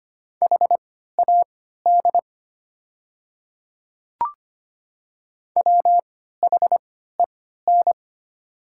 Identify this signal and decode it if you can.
Morse code, heard as a keyed tone, about 25 wpm, spaced as for 14 wpm, 690 Hz. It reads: HAD WHEN
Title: 3x Character Spacing - Top 100 Words - 25wpm